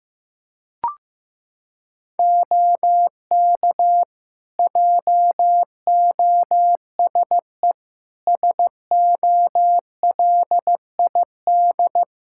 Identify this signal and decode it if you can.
Morse code, heard as a keyed tone, about 15 wpm, 700 Hz. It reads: OK JOSE SOLID
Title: Practice QSO - Exchange 3 - More Info - 15wpm